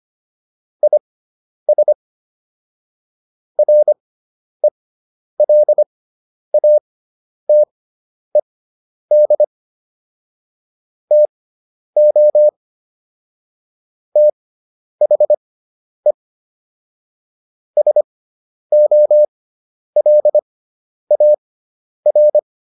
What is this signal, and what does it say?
Morse code, heard as a keyed tone, about 25 wpm, spaced as for 10 wpm, 600 Hz. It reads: IS RELATED TO THE SOLAR